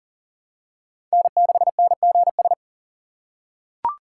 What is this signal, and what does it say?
Morse code, heard as a keyed tone, about 37 wpm, 700 Hz. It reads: N6NGS
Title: USA Call Signs - 5 Characters - 6x Character Spacing - 40wpm